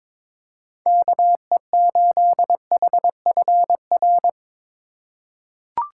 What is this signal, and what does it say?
Morse code, heard as a keyed tone, about 21 wpm, 700 Hz. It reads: KE8HFR
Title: USA Call Signs - 4x Character Spacing - 22wpm